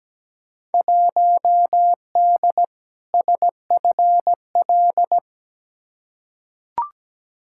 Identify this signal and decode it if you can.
Morse code, heard as a keyed tone, about 17 wpm, 700 Hz. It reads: 1D SFL
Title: Field Day Exchanges - 17wpm